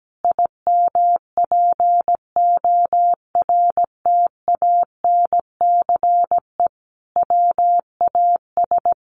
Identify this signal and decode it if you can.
Morse code, heard as a keyed tone, about 17 wpm, 700 Hz. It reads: IMPORTANCE WAS
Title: Sentences from Top 1400 Words 17wpm